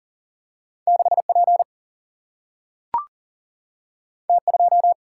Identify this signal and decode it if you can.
Morse code, heard as a keyed tone, about 39 wpm, 700 Hz. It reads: 6P T2
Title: Single-Letter Single-Number 40wpm